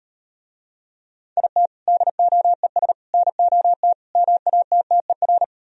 Morse code, heard as a keyed tone, about 38 wpm, 700 Hz. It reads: IT DOES NOT MATTER